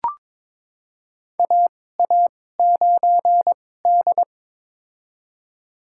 Morse code, heard as a keyed tone, about 22 wpm, spaced as for 16 wpm, 700 Hz. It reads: AA9D
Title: USA Call Signs - 2x Character Spacing - 22wpm